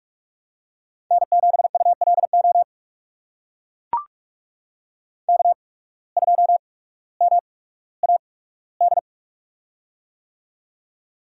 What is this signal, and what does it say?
Morse code, heard as a keyed tone, about 44 wpm, spaced as for 19 wpm, 700 Hz. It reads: N7ULO K2MAD
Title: USA Call Signs - 5 Characters - 8x Character Spacing - 45wpm